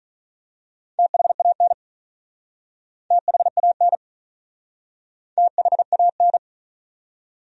Morse code, heard as a keyed tone, about 43 wpm, 700 Hz. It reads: THAN THAN THAN
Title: Speed-Racing Top 500 Words 35wpm